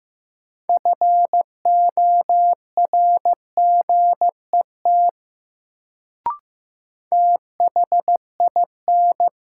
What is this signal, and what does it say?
Morse code, heard as a keyed tone, about 15 wpm, 700 Hz. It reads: FORGET THIN
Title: Sentences from Top 1200 Words - New Word Drill - 15wpm